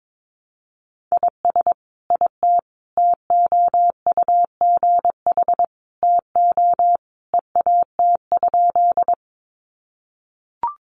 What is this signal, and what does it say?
Morse code, heard as a keyed tone, about 22 wpm, 700 Hz. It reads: IS IT TOUGH TO EAT?
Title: Sentences from Top 1200 Words 22wpm